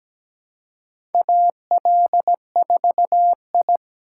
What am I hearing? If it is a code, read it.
Morse code, heard as a keyed tone, about 17 wpm, 700 Hz. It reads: AL4I
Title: USA Call Signs - 4 Characters - 4x Character Spacing - 17wpm